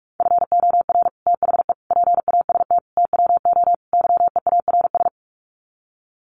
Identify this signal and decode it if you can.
Morse code, heard as a keyed tone, about 44 wpm, 700 Hz. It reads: FOR THE PAST TWO YEARS
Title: Sets of 5 Words Encore2 45wpm